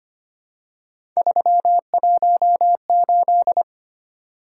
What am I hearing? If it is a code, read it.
Morse code, heard as a keyed tone, about 25 wpm, 700 Hz. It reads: 318
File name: Three Letter-Number Combinations - 25wpm